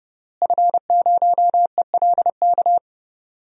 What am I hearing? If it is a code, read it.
Morse code, heard as a keyed tone, about 32 wpm, 700 Hz. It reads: F0ELK
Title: DXCC-100 Call Signs - 5 Characters - Repeated 2x - 30wpm